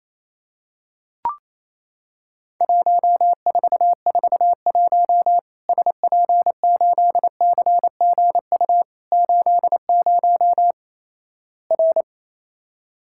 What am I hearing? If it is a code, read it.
Morse code, heard as a keyed tone, about 28 wpm, 700 Hz. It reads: 1441 SP8CGU 80 R
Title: Worked All Europe DX Contest - QTC Exchange - 28wpm